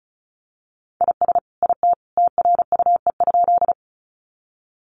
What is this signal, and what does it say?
Morse code, heard as a keyed tone, about 34 wpm, 700 Hz. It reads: IS IT TRUE?
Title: Sentences from Top 400 Words 35wpm